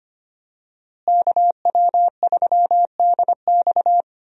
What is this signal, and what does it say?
Morse code, heard as a keyed tone, about 25 wpm, 700 Hz. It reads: KW3DX
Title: USA Call Signs - 5 Characters - Repeated 3x - 25wpm